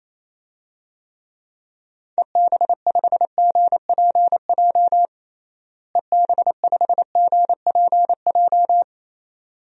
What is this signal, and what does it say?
Morse code, heard as a keyed tone, about 28 wpm, 700 Hz. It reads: EB5GPJ EB5GPJ
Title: DXCC-100 Call Signs - 6 Characters - Repeated 2x - 28wpm